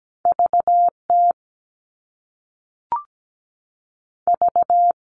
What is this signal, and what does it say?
Morse code, heard as a keyed tone, about 17 wpm, 700 Hz. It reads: VT V
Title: US States 17wpm